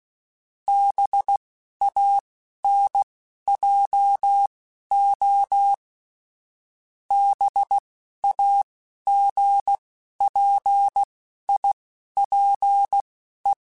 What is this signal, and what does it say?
Morse code, heard as a keyed tone, about 16 wpm, spaced as for 12 wpm, 800 Hz. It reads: BANJO BAGPIPE